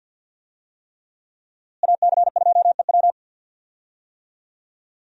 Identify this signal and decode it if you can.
Morse code, heard as a keyed tone, about 52 wpm, 700 Hz. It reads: AC2EW
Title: USA Call Signs - 5 Characters - Repeated 3x - 50wpm